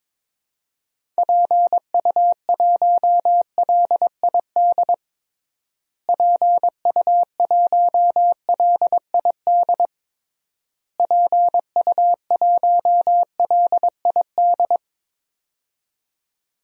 Morse code, heard as a keyed tone, about 22 wpm, 700 Hz. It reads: PU1LID PU1LID PU1LID
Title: DXCC-100 Call Signs - 6 Characters - Repeated 3x - 22wpm